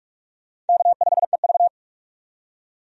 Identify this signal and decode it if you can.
Morse code, heard as a keyed tone, about 43 wpm, 700 Hz. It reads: K5EV
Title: USA Call Signs - 4 Characters - 45wpm